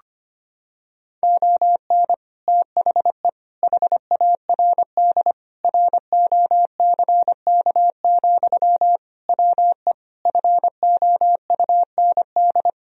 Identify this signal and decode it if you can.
Morse code, heard as a keyed tone, about 25 wpm, 700 Hz. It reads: ON THE HARD ROCK, WE FOUND
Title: Sentences from Top 1000 Words - Review Part 2 - 25wpm